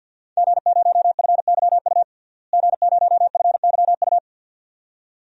Morse code, heard as a keyed tone, about 48 wpm, 700 Hz. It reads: G0FYU G0FYU
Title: DXCC-100 Call Signs - 5 Characters - Repeated 2x - 50wpm